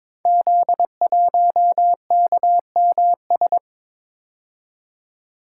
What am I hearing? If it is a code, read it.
Morse code, heard as a keyed tone, about 22 wpm, 700 Hz. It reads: Z1KMS
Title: Code Groups - 5 Letter:Numbers - 22wpm